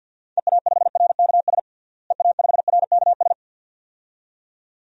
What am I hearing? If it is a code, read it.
Morse code, heard as a keyed tone, about 52 wpm, 700 Hz. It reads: EA5RKS EA5RKS
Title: DXCC-100 Call Signs - 6 Characters - Repeated 2x - 50wpm